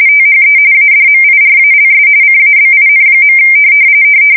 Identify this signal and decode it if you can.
RTTY, about 45.45 baud. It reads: THE ADMIRALS CABIN.
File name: fsk170 / battleshipnj